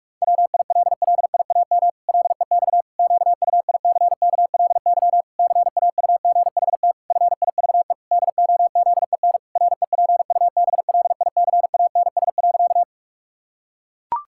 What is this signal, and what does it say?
Morse code, heard as a keyed tone, about 46 wpm, 700 Hz. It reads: WILLIAM LEX QUICKLY CAUGHT FIVE DOZEN REPUBLICANS.
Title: Pangram Challenge v2 - 45wpm